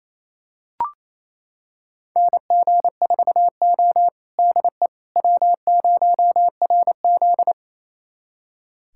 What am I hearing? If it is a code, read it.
Morse code, heard as a keyed tone, about 28 wpm, 700 Hz. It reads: NG4O DE W0RZ
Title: USA Callsign DE Callsign - 4 Characters - 28wpm